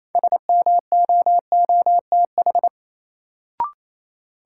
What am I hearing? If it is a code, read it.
Morse code, heard as a keyed tone, about 28 wpm, 700 Hz. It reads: SMOOTH
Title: Sentences from Top 1400 Words - New Word Drill - 28wpm